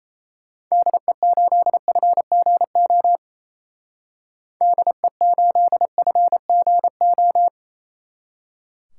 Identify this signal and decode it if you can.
Morse code, heard as a keyed tone, about 34 wpm, 700 Hz. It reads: DE8FGO DE8FGO
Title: Speed Racing - USA 2x3 Call Signs 28wpm